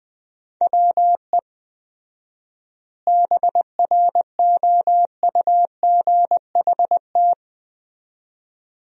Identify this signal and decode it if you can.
Morse code, heard as a keyed tone, about 21 wpm, 700 Hz. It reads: WE BROUGHT
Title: 4x Word-Spacing - Sentences from Top 500 Words - 20wpm